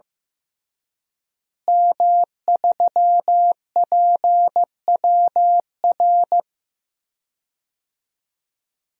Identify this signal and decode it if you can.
Morse code, heard as a keyed tone, about 15 wpm, 700 Hz. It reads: M3PWR
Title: DXCC-100 Call Signs - 5 Characters - Repeated 2x - 15wpm